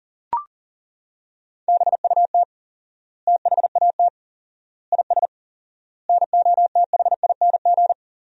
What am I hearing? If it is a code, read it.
Morse code, heard as a keyed tone, about 38 wpm, 700 Hz. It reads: BUT THAT IS NOTHING
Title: 4x Word-Spacing - Sentences from Top 500 Words - 40wpm